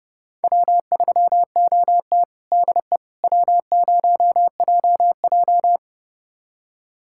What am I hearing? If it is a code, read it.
Morse code, heard as a keyed tone, about 30 wpm, 700 Hz. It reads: W3OT DE W0JJ
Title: USA Callsign DE Callsign - 4 Characters - 30wpm